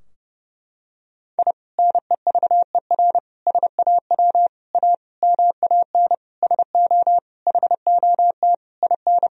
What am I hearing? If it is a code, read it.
Morse code, heard as a keyed tone, about 30 wpm, 700 Hz. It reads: I NEVER SAW A MAN SO HOT IN
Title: Sentences from Top 1000 Words - Review Part 3 - 30wpm